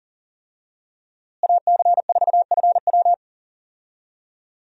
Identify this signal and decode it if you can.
Morse code, heard as a keyed tone, about 40 wpm, 700 Hz. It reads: AC4FW
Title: USA Call Signs - 5 Characters - Repeated 3x - 40wpm